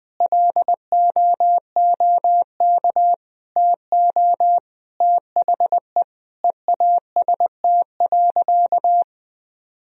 Morse code, heard as a keyed tone, about 20 wpm, 700 Hz. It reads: LOOK TO THE EAST.
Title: Sentences from Top 1000 Words - Review Part 3 - 20wpm